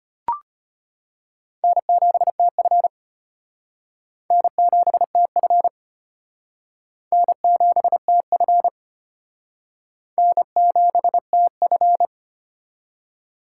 Morse code, heard as a keyed tone, about 31 wpm, 700 Hz. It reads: N7TF N7TF N7TF N7TF
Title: Speed Racing - USA 1x2 Call Signs 25wpm